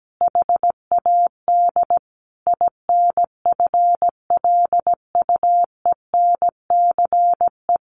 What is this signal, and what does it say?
Morse code, heard as a keyed tone, about 17 wpm, 700 Hz. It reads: HAD INFLUENCE
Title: Sentences from Top 1100 Words 17wpm